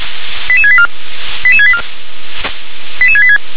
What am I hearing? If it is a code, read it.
Five-tone selective call (ZVEI), heard as quick runs of five tones, tones of about 70 ms. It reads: 88684 88684 88686
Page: SDR/ZVEI